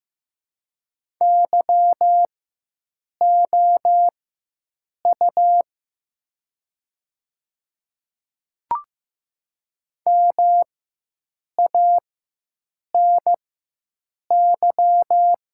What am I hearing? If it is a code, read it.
Morse code, heard as a keyed tone, about 15 wpm, spaced as for 7 wpm, 700 Hz. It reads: YOU MANY
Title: 4x Character Spacing - Top 300 Words - 15wpm